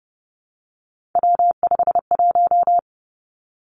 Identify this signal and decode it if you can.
Morse code, heard as a keyed tone, about 30 wpm, 700 Hz. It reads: W51
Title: Three Challenging Letter-Number Combinations 30wpm